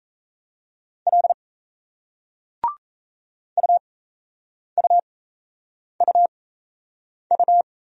Morse code, heard as a keyed tone, about 38 wpm, spaced as for 8 wpm, 700 Hz. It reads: L UUUU